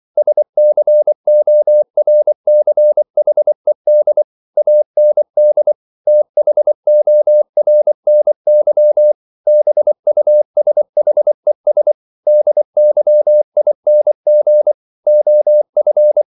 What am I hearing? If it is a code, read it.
Morse code, heard as a keyed tone, about 24 wpm, 600 Hz. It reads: SCORCHED AND THORNY BUSHES DYING OF